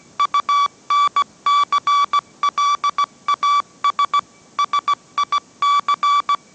Morse code, heard as a keyed tone, about 17 wpm, 1150 Hz. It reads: UNCLAS SIC
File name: mrx / mrx080 18wpm